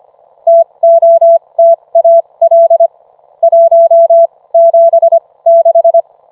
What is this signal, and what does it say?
Morse code, heard as a keyed tone, about 25 wpm, spaced as for 20 wpm, 670 Hz. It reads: TOTAL 176